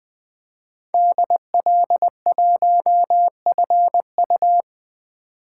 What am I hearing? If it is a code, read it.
Morse code, heard as a keyed tone, about 20 wpm, 700 Hz. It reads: DL1FU
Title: 2023 SOTA Chaser Call Signs - 20wpm